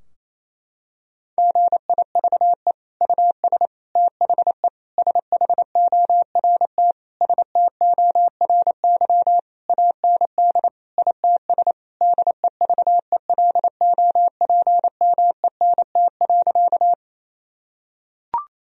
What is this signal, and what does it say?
Morse code, heard as a keyed tone, about 28 wpm, 700 Hz. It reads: GIVE US THE SHORT STORY AND ITS DEVELOPMENT.
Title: Sentences from Top 1000 Words - Review Part 4 - 28wpm